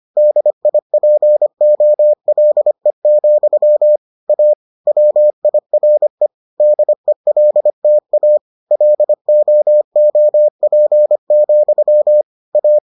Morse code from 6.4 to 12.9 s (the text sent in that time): DELTA LOOP, A